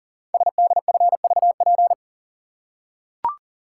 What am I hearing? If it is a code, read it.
Morse code, heard as a keyed tone, about 38 wpm, 700 Hz. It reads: SDFVP
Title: Code Groups - 5 Letter:Numbers - 40wpm